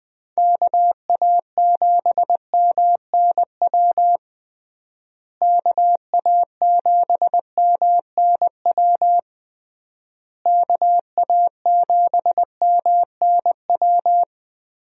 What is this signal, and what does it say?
Morse code, heard as a keyed tone, about 20 wpm, 700 Hz. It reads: KA7MNW KA7MNW KA7MNW
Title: USA Call Signs - 6 Characters - Repeated 3x - 20wpm